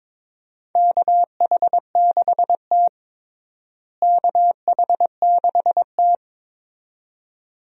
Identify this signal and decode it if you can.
Morse code, heard as a keyed tone, about 22 wpm, 700 Hz. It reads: KH6T KH6T
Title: DXCC-100 Call Signs - 4 Characters - Repeated 2x - 22wpm